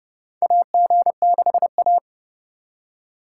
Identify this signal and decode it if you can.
Morse code, heard as a keyed tone, about 29 wpm, 700 Hz. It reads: AG6A
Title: USA Call Signs - 4 Characters - 4x Character Spacing - 30wpm